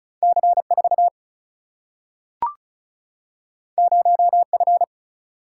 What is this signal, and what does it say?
Morse code, heard as a keyed tone, about 36 wpm, 700 Hz. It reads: C4 0F
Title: Single-Letter Single-Number 35wpm